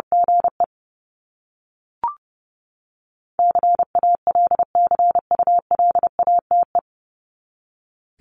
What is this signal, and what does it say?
Morse code, heard as a keyed tone, about 30 wpm, 700 Hz. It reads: GE CALCULATE
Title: Top 1751-2000 Words 30wpm